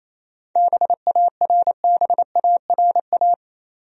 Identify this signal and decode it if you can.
Morse code, heard as a keyed tone, about 28 wpm, 700 Hz. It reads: BARBARA
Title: USA Common Names - 28wpm